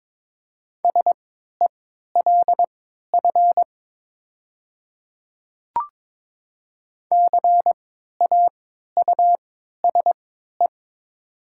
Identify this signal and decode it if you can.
Morse code, heard as a keyed tone, about 22 wpm, spaced as for 12 wpm, 700 Hz. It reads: SELF CAUSE
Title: 3x Character Spacing - Top 300 Words - 22wpm